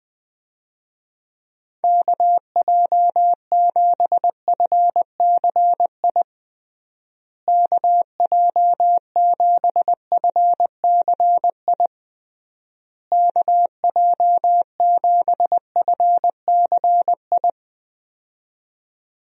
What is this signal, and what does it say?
Morse code, heard as a keyed tone, about 20 wpm, 700 Hz. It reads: KJ7FCI KJ7FCI KJ7FCI
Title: USA Call Signs - 6 Characters - Repeated 3x - 20wpm